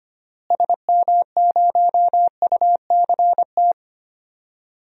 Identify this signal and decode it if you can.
Morse code, heard as a keyed tone, about 25 wpm, 700 Hz. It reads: SM0UCT
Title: DXCC-100 Call Signs - 6 Characters - Repeated 3x - 25wpm